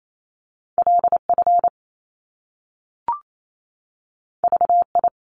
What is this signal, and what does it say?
Morse code, heard as a keyed tone, about 28 wpm, 700 Hz. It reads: LF VI
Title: Top Two Letter Combinations in Words 28wpm